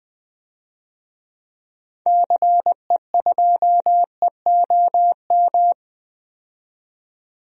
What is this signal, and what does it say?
Morse code, heard as a keyed tone, about 20 wpm, 700 Hz. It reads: CE2EOM
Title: DXCC-100 Call Signs - 6 Characters - 20wpm